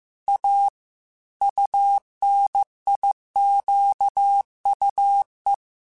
Morse code, heard as a keyed tone, about 15 wpm, 800 Hz. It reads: A UNIQUE
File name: Technical Subjects / Technical Subjects-mp3s / Gates Intro @13